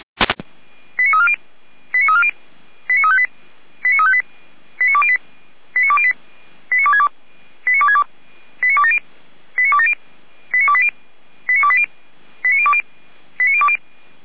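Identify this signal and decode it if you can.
Five-tone selective call (ZVEI), heard as quick runs of five tones, tones of about 70 ms. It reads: 89340 89340 89358 89358 89208 89208 89262 89262 89270 89270 89280 89280 89020 89020
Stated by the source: ZVEI five-tone selective call received on an SDR.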